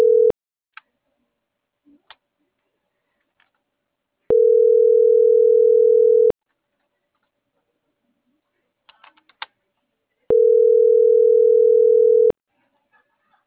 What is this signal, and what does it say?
Ringback tone (North America), 440 + 480 Hz.